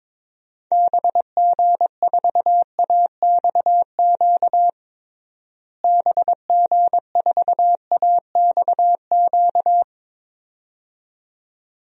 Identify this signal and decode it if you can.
Morse code, heard as a keyed tone, about 22 wpm, 700 Hz. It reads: BG4AXQ BG4AXQ
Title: DXCC-100 Call Signs - 6 Characters - Repeated 2x - 22wpm